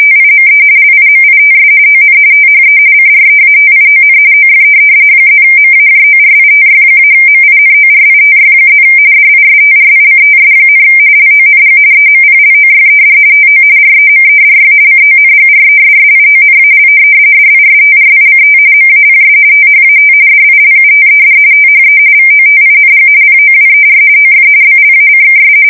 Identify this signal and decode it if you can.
RTTY, about 45.45 baud. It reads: TIME LIMITS ON THE CREATION OF +MAJORITY-MINORITY+ DISTRICTS, LONG EMPLOYED TO REMEDY CONGRESSIONAL DISTRICT MAPS THAT DILUTED THE VOTING POWER OF BLACKS
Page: 170 Shift 2125/2295